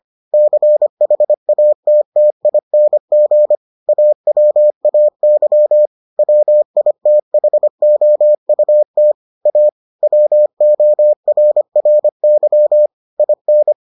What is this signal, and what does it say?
Morse code, heard as a keyed tone, about 25 wpm, 600 Hz. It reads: CHATTING AWAY WITHOUT A WORRY IN